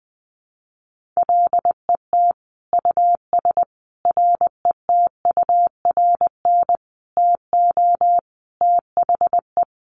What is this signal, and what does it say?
Morse code, heard as a keyed tone, about 20 wpm, 700 Hz. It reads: LET US RETURN TO THE
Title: Sentences from Top 1500 Words 20wpm